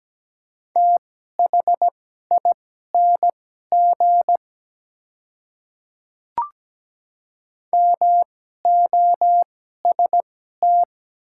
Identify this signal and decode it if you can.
Morse code, heard as a keyed tone, about 17 wpm, spaced as for 12 wpm, 700 Hz. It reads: THING MOST
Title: 2x Character Spacing - Top 100 Words - 17wpm